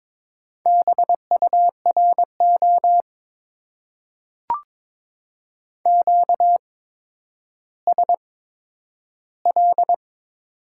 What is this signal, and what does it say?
Morse code, heard as a keyed tone, about 22 wpm, spaced as for 11 wpm, 700 Hz. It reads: BURO QSL
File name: Top 90 QSO Elements - 8x Character Spacing - 22wpm